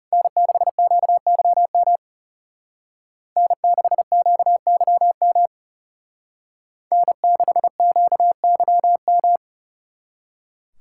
Morse code, heard as a keyed tone, about 36 wpm, 700 Hz. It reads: N6QYM N6QYM N6QYM
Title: Speed Racing - USA Call Signs 30wpm